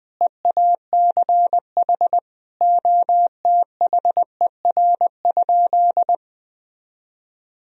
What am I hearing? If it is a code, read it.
Morse code, heard as a keyed tone, about 20 wpm, 700 Hz. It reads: EACH OTHER?
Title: Sentences from Top 200 Words and Abbreviations 20wpm